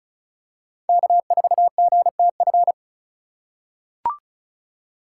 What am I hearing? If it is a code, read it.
Morse code, heard as a keyed tone, about 36 wpm, 700 Hz. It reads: K4GTF
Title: USA Call Signs - 5 Characters - 3x Character Spacing - 35wpm